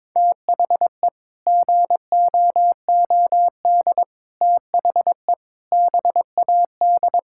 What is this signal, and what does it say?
Morse code, heard as a keyed tone, about 22 wpm, 700 Hz. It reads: THE GOOD THE BAD
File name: practice / 22wpm / Sets of 6 Words - 22wpm